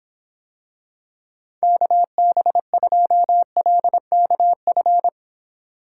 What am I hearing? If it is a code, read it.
Morse code, heard as a keyed tone, about 26 wpm, 700 Hz. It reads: KB2LKF